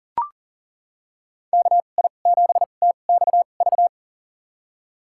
Morse code, heard as a keyed tone, about 37 wpm, spaced as for 28 wpm, 700 Hz. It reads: KI7TXV
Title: USA Call Signs - 6 Characters - 2x Character Spacing - 40wpm